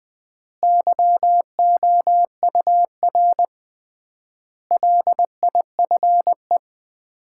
Morse code, heard as a keyed tone, about 20 wpm, 700 Hz. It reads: YOUR LIFE
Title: 3x Word-Spacing - Sets of 5 Words - 20wpm